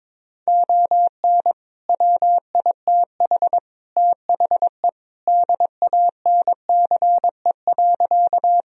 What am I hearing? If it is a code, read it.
Morse code, heard as a keyed tone, about 22 wpm, 700 Hz. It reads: ON WITH THE DANCE.
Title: Sentences from Top 1000 Words - Review Part 4 - 22wpm